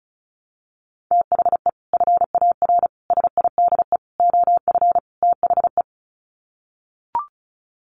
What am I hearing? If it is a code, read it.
Morse code, heard as a keyed tone, about 34 wpm, 700 Hz. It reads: THE FAR SIDE OF THE
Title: Sets of 5 Words 35wpm